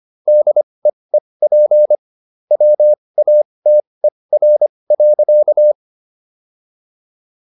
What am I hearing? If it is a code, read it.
Morse code, heard as a keyed tone, about 25 wpm, spaced as for 20 wpm, 600 Hz. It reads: DEEP WATER.